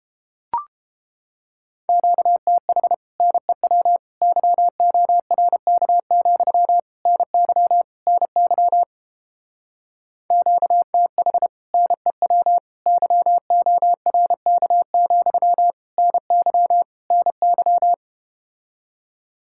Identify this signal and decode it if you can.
Morse code, heard as a keyed tone, about 31 wpm, 700 Hz. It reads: QTH NEW YORK, NY NY QTH NEW YORK, NY NY